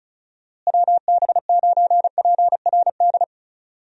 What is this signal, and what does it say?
Morse code, heard as a keyed tone, about 34 wpm, 700 Hz. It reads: WB9PRD
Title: USA Call Signs - 6 Characters - Repeated 2x - 35wpm